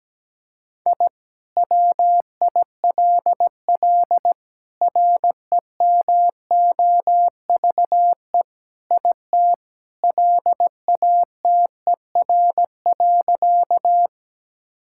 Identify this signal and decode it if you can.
Morse code, heard as a keyed tone, about 17 wpm, 700 Hz. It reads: I WILL REMOVE IT LATER.
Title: Sentences from Top 1000 Words - Review Part 5 - 17wpm